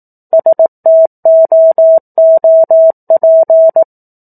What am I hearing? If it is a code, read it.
Morse code, heard as a keyed tone, about 18 wpm, 650 Hz. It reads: STOOP
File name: 04-Chapter-04 18 WPM